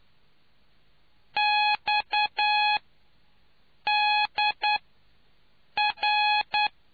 Morse code, heard as a keyed tone, about 9 wpm, 840 Hz. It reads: X D R